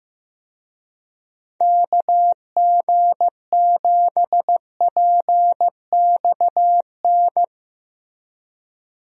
Morse code, heard as a keyed tone, about 15 wpm, 700 Hz. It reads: KG7PXN
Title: USA Call Signs - 6 Characters - Repeated 2x - 15wpm